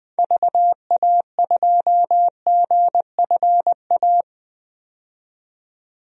Morse code, heard as a keyed tone, about 20 wpm, 700 Hz. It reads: VA2GFA
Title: DXCC-100 Call Signs - 6 Characters - 20wpm